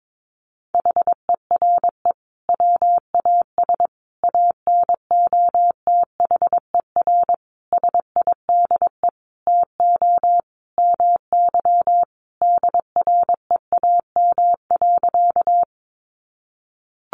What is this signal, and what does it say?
Morse code, heard as a keyed tone, about 22 wpm, 700 Hz. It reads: HERE WAS ANOTHER SIDE TO MY DREAM.
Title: Sentences from Top 1500 Words 22wpm